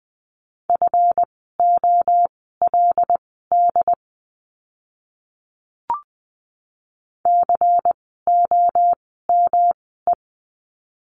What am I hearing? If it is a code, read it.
Morse code, heard as a keyed tone, about 20 wpm, spaced as for 14 wpm, 700 Hz. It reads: FOLD COME
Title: 139 Lesson 23f - Letter f - Words 20wpm